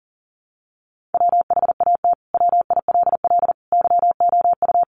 Morse code, heard as a keyed tone, about 40 wpm, 700 Hz. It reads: WHAT WILL YOU
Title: Sentences from Top 1300 Words 40wpm